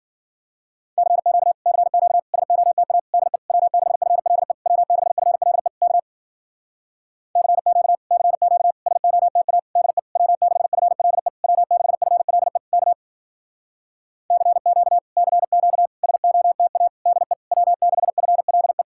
Morse code, heard as a keyed tone, about 51 wpm, 700 Hz. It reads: CQ CQ SOTA DE W6FLE W6FLE K CQ CQ SOTA DE W6FLE W6FLE K CQ CQ SOTA DE W6FLE